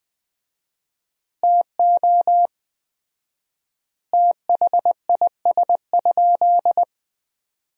Morse code, heard as a keyed tone, about 20 wpm, 700 Hz. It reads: TO THIS?
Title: 4x Word-Spacing - Sentences from Top 100 Words - 20wpm